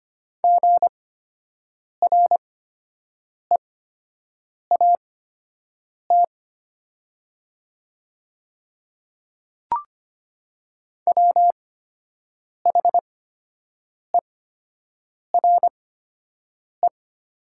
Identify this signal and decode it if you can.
Morse code, heard as a keyed tone, about 25 wpm, spaced as for 7 wpm, 700 Hz. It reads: GREAT WHERE
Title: 8x Character Spacing - Top 300 Words - 25wpm